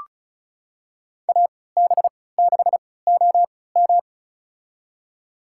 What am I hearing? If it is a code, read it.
Morse code, heard as a keyed tone, about 36 wpm, spaced as for 20 wpm, 700 Hz. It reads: AB6OM
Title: USA Call Signs - 5 Characters - 3x Character Spacing - 35wpm